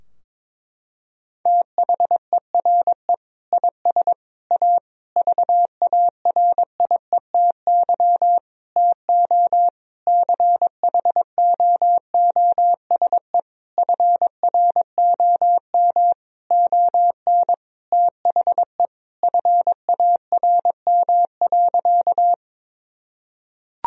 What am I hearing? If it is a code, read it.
Morse code, heard as a keyed tone, about 22 wpm, 700 Hz. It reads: THERE IS A VARIETY TO CHOOSE FROM ON THE FARM.
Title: Sentences from Top 1000 Words - Review Part 4 - 22wpm